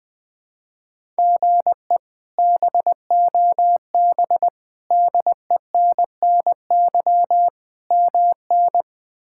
Morse code, heard as a keyed tone, about 21 wpm, 700 Hz. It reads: GE BOB DENNY MN
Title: K1USN SST QSO Practice - 20wpm